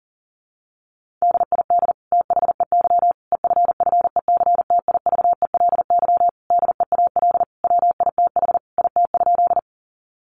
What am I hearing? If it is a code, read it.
Morse code, heard as a keyed tone, about 37 wpm, 700 Hz. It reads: DID THEY EFFECTIVELY DEAL WITH IT?
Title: Sentences from Top 1400 Words 40wpm